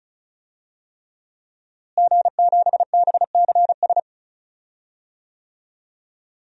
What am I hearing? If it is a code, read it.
Morse code, heard as a keyed tone, about 34 wpm, 700 Hz. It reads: G7BCS
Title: DXCC-100 Call Signs - 5 Characters - Repeated 3x - 35wpm